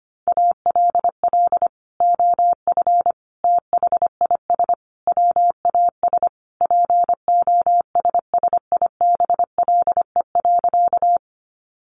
Morse code, heard as a keyed tone, about 25 wpm, 700 Hz. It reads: ALL OF THIS WAS POSSIBLE.
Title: Sentences from Top 500 Words 25wpm